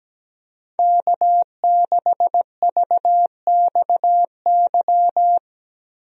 Morse code, heard as a keyed tone, about 17 wpm, 700 Hz. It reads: K6VXY
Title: USA Call Signs - 5 Characters - 17wpm